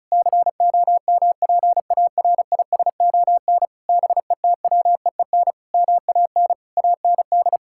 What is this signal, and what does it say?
Morse code, heard as a keyed tone, about 36 wpm, 700 Hz. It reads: COMPARISON BETWEEN MAN AND
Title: Sentences from Top 1600 Words 35wpm